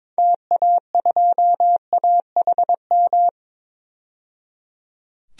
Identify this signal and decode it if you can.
Morse code, heard as a keyed tone, about 21 wpm, 700 Hz. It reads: TA2AHM
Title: DXCC-100 Call Signs - 6 Characters - 22wpm